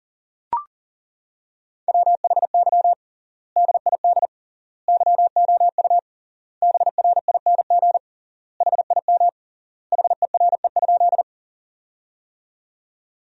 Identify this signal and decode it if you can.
Morse code, heard as a keyed tone, about 40 wpm, 700 Hz. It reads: WHY DID YOU BRING HIM HERE?
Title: 3x Word-Spacing - Sentences from Top 500 Words - 40wpm